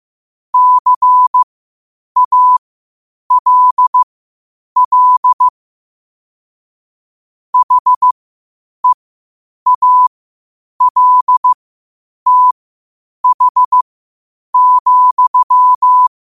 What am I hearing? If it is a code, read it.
Morse code, heard as a keyed tone, about 15 wpm, spaced as for 8 wpm, 1000 Hz. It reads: CALL HEALTH,
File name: qotdfiles / 8wpm